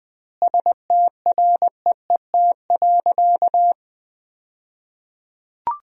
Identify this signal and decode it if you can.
Morse code, heard as a keyed tone, about 20 wpm, 700 Hz. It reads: STREET.
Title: 2x Word-Spacing - Sentences from Top 500 Words - 20wpm